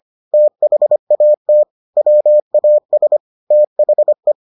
Morse code, heard as a keyed tone, about 25 wpm, 600 Hz. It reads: THAT WAS THE